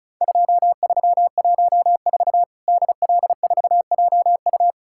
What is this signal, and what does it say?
Morse code, heard as a keyed tone, about 36 wpm, 700 Hz. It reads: 2314 DL4JU
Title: Worked All Europe DX Contest - QTC Exchange - 35wpm